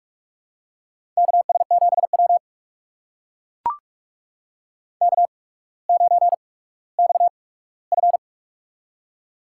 Morse code, heard as a keyed tone, about 44 wpm, spaced as for 19 wpm, 700 Hz. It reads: KS7W K9XF
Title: USA Call Signs - 4 Characters - 8x Character Spacing - 45wpm